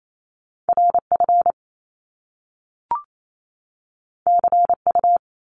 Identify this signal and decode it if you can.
Morse code, heard as a keyed tone, about 28 wpm, 700 Hz. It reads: RF CU
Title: Top Two Letter Combinations in Words 28wpm